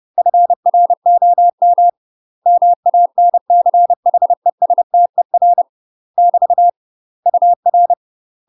Morse code, heard as a keyed tone, about 30 wpm, 700 Hz. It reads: FROM MANCHESTER <BT> UR
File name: UK MASTER QSOs TRAINING / 30WPM / UK MASTER QSOs 0002